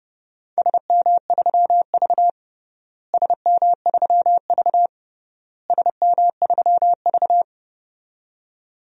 Morse code, heard as a keyed tone, about 32 wpm, 700 Hz. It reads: SM3V SM3V SM3V
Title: DXCC-100 Call Signs - 4 Characters - Repeated 3x - 30wpm